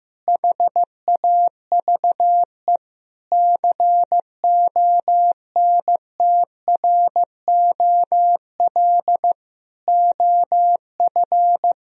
Morse code, heard as a keyed tone, about 15 wpm, 700 Hz. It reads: HAVE CONTROL OF
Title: Sentences from Top 1000 Words - Review Part 3 - 15wpm